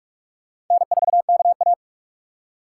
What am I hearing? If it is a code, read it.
Morse code, heard as a keyed tone, about 44 wpm, 700 Hz. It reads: N4KA